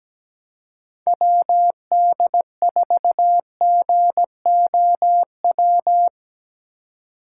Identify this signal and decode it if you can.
Morse code, heard as a keyed tone, about 17 wpm, 700 Hz. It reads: WD4GOW
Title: USA Call Signs - 3x Character Spacing - 17wpm